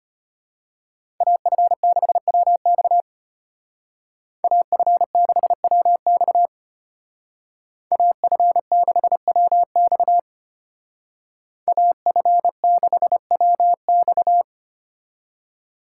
Morse code, heard as a keyed tone, about 30 wpm, 700 Hz. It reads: AF6WX AF6WX AF6WX AF6WX
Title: Speed Racing - USA 2x2 Call Signs 25wpm